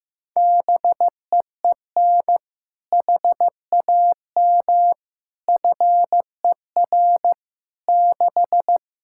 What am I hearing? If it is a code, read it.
Morse code, heard as a keyed tone, about 15 wpm, 700 Hz. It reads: BEEN HAM FER 6
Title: K1USN SST QSO Practice 15wpm